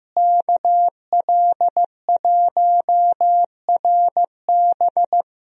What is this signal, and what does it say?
Morse code, heard as a keyed tone, about 15 wpm, 700 Hz. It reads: KL1RB
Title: USA Call Signs - 5 Characters - Repeated 3x - 15wpm